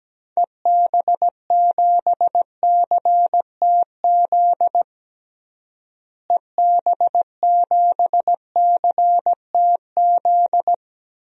Morse code, heard as a keyed tone, about 17 wpm, 700 Hz. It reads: EB7CTZ EB7CTZ
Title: DXCC-100 Call Signs - 6 Characters - Repeated 3x - 17wpm